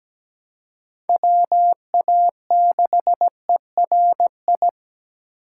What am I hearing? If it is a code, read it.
Morse code, heard as a keyed tone, about 20 wpm, 700 Hz. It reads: WA6ERI
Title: Speed Racing - USA Call Signs 17wpm